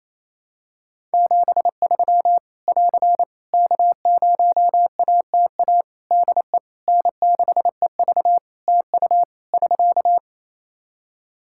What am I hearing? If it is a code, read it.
Morse code, heard as a keyed tone, about 28 wpm, 700 Hz. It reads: 73 <AR> K0ATA DE N6EV TU <SK>
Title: Practice QSO - Exchange 4 - Ending - 28wpm